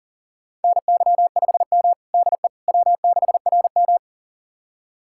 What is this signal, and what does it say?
Morse code, heard as a keyed tone, about 37 wpm, 700 Hz. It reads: NY5M DE W6RM
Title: USA Callsign DE Callsign - 4 Characters - 40wpm